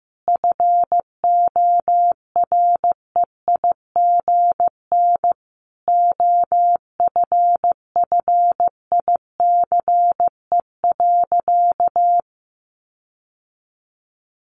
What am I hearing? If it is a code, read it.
Morse code, heard as a keyed tone, about 15 wpm, 700 Hz. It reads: FOREIGN OFFICE.
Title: Sentences from Top 1400 Words 15wpm